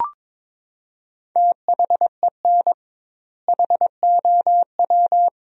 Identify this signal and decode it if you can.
Morse code, heard as a keyed tone, about 22 wpm, 700 Hz. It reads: THEN HOW
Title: 2x Word-Spacing - Sentences from Top 100 Words - 22wpm